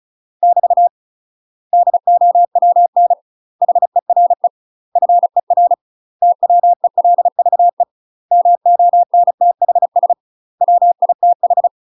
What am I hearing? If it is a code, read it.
Morse code, heard as a keyed tone, about 35 wpm, 700 Hz. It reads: <BT> DOWN HERE FER TWELVE MONTHS WITH
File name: UK MASTER QSOs TRAINING / 35WPM / UK MASTER QSOs 0023